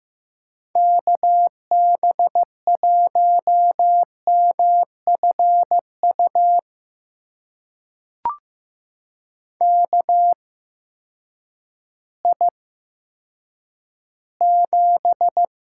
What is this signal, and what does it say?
Morse code, heard as a keyed tone, about 15 wpm, spaced as for 9 wpm, 700 Hz. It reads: KB1MFU KI7
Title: USA Call Signs - 6 Characters - 8x Character Spacing - 15wpm